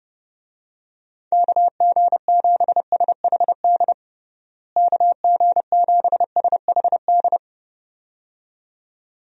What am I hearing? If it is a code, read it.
Morse code, heard as a keyed tone, about 30 wpm, 700 Hz. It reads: KG7SHD KG7SHD
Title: USA Call Signs - 6 Characters - Repeated 2x - 30wpm